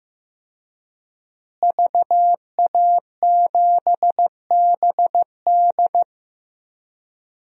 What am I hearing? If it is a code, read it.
Morse code, heard as a keyed tone, about 15 wpm, 700 Hz. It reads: VA7BD
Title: DXCC-100 Call Signs - 5 Characters - Repeated 2x - 15wpm